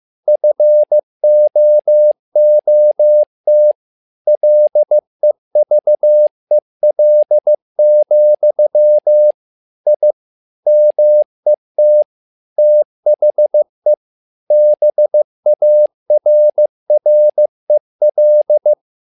Morse code, heard as a keyed tone, about 15 wpm, 600 Hz. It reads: FOOT LEVEL, I MET THE BARREL
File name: learning center / tower trouble / tower trouble 15wpm